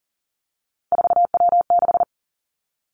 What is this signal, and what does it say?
Morse code, heard as a keyed tone, about 40 wpm, 700 Hz. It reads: 4W6
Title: Three Challenging Letter-Number Combinations 40wpm